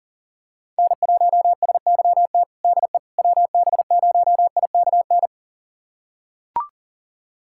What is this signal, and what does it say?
Morse code, heard as a keyed tone, about 40 wpm, 700 Hz. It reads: N1SYT DE WB0IKN
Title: USA Callsign DE Callsign 40wpm